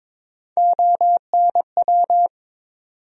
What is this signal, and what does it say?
Morse code, heard as a keyed tone, about 22 wpm, 700 Hz. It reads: ONW
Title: USA Call Sign Postfixes - 3 Letters 22wpm